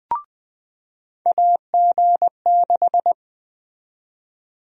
Morse code, heard as a keyed tone, about 20 wpm, 700 Hz. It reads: AG6
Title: USA Call Sign Prefixes - 20wpm